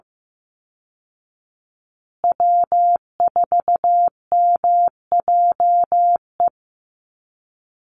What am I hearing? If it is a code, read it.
Morse code, heard as a keyed tone, about 15 wpm, 700 Hz. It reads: W4MJE
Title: Call Signs Repeated 2x - 15wpm